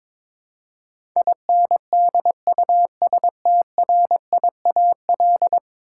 Morse code, heard as a keyed tone, about 22 wpm, 700 Hz. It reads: INDUSTRIAL
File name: practice / 22wpm / 10 Letter Words - First 3 Letters Given - 22wpm